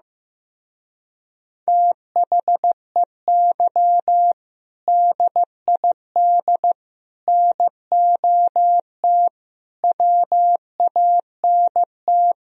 Morse code, heard as a keyed tone, about 15 wpm, 700 Hz. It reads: THEY DID NOT WANT
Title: Speed-Racing - Sets of 5 Words 15wpm